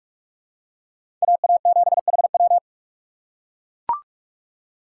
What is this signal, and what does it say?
Morse code, heard as a keyed tone, about 43 wpm, 700 Hz. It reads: AA7HW
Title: USA Call Signs - 5 Characters - 3x Character Spacing - 45wpm